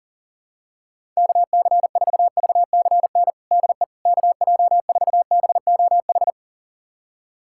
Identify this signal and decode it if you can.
Morse code, heard as a keyed tone, about 37 wpm, 700 Hz. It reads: KC4VCN DE KJ4BOH
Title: USA Callsign DE Callsign - 6 Characters - 40wpm